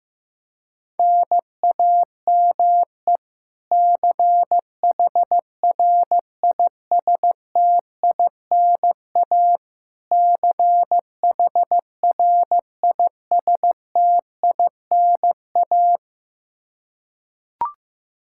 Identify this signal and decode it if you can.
Morse code, heard as a keyed tone, about 16 wpm, 700 Hz. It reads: NAME CHRISTINA CHRISTINA
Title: K1USN SST QSO Practice 15wpm